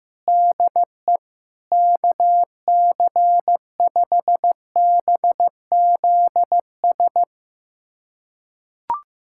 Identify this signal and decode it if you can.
Morse code, heard as a keyed tone, about 15 wpm, 700 Hz. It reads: DE KC5BZS
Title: USA Callsign DE Callsign 15wpm